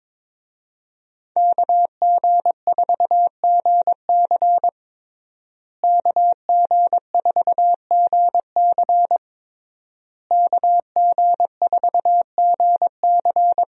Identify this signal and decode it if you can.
Morse code, heard as a keyed tone, about 22 wpm, 700 Hz. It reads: KG4GC KG4GC KG4GC
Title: USA Call Signs - 5 Characters - Repeated 3x - 22wpm